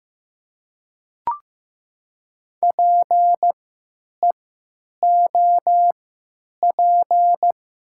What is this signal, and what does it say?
Morse code, heard as a keyed tone, about 15 wpm, spaced as for 9 wpm, 700 Hz. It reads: PEOP
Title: 3x Character Spacing - Top 300 Words - 15wpm